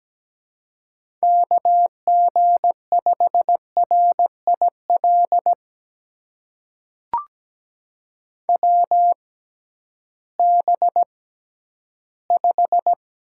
Morse code, heard as a keyed tone, about 17 wpm, spaced as for 11 wpm, 700 Hz. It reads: KG5RIL WB5
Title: USA Call Signs - 6 Characters - 6x Character Spacing - 17wpm